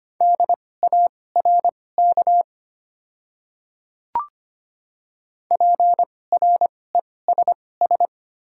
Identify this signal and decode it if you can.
Morse code, heard as a keyed tone, about 25 wpm, spaced as for 18 wpm, 700 Hz. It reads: DARK PRESS
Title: 2x Character Spacing - Top 500 Words - 25wpm